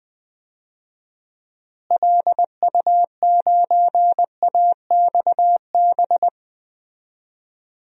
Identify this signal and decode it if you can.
Morse code, heard as a keyed tone, about 21 wpm, 700 Hz. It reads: LU9AXB
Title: DXCC-100 Call Signs - 6 Characters - Repeated 2x - 20wpm